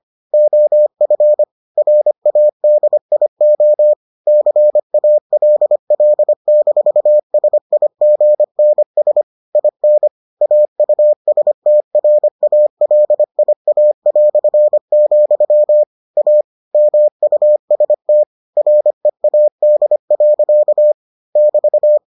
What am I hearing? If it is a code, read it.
Morse code, heard as a keyed tone, about 25 wpm, 600 Hz. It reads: OF RADIO CALL-SIGNS IN AUSTRALIA", A MUST READ. <BT>